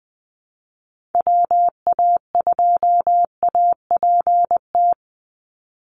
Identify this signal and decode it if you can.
Morse code, heard as a keyed tone, about 20 wpm, 700 Hz. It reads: WA2APT
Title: USA Call Signs 20wpm